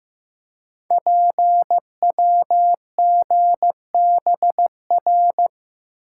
Morse code, heard as a keyed tone, about 15 wpm, 690 Hz. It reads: PWGBR
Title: Code Groups - 5 Letters - 15wpm